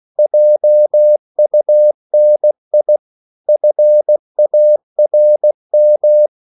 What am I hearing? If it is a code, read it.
Morse code, heard as a keyed tone, about 16 wpm, 600 Hz. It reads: JUNI FARM
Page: Korte danske ord 16wpm